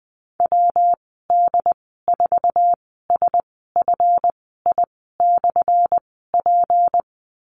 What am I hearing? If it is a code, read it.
Morse code, heard as a keyed tone, about 20 wpm, spaced as for 14 wpm, 700 Hz. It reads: WD4SFI/P
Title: 179 Lesson 29g - Slash - Call Signs 20wpm